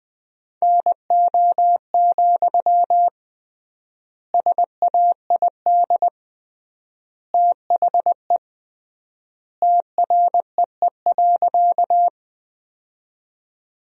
Morse code, heard as a keyed tone, about 21 wpm, 700 Hz. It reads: NO, SAID THE TREE.
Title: 3x Word-Spacing - Sentences from Top 300 Words - 20wpm